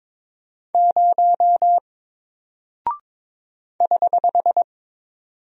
Morse code, heard as a keyed tone, about 21 wpm, 700 Hz. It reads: 0 <HH>
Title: Single Character (Letter:Number:Punctuation:Prosign) Alt 22wpm